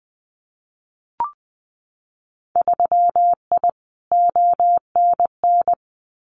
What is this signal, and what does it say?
Morse code, heard as a keyed tone, about 20 wpm, 700 Hz. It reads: 3I ONN
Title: Winter Field Day Exchanges Encore 20wpm